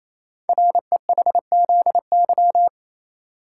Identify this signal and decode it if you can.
Morse code, heard as a keyed tone, about 27 wpm, 700 Hz. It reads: REHZY
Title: Code Groups - 5 Letters - 28wpm